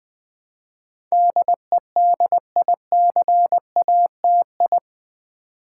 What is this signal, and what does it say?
Morse code, heard as a keyed tone, about 20 wpm, 700 Hz. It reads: DEDICATI
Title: Build A Word - 4 Letters At A Time - 10-Letter Words - 20wpm